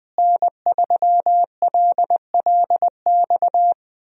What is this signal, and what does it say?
Morse code, heard as a keyed tone, about 20 wpm, 700 Hz. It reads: N3LLX
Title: USA Call Signs - 5 Characters - Repeated 3x - 20wpm